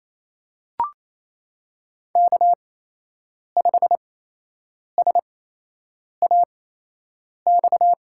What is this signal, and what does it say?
Morse code, heard as a keyed tone, about 28 wpm, spaced as for 8 wpm, 700 Hz. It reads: K5SAX
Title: USA Call Signs - 5 Characters - 8x Character Spacing - 28wpm